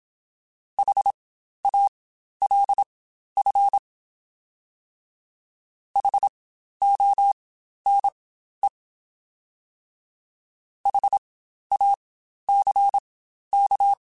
Morse code, heard as a keyed tone, about 27 wpm, spaced as for 12 wpm, 800 Hz. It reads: HALF HONE HACK
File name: Spaced Alphas